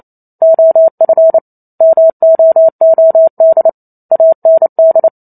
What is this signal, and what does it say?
Morse code, heard as a keyed tone, about 28 wpm, 650 Hz. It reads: OF MOOD AND